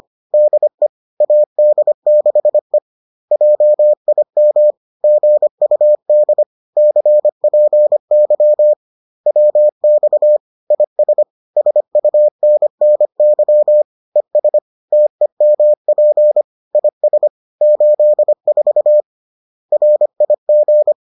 Morse code, heard as a keyed tone, about 25 wpm, 600 Hz. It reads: DE AD6E JIM GUD CPY WX IS SUNNY ES TEMP IS 84 RIG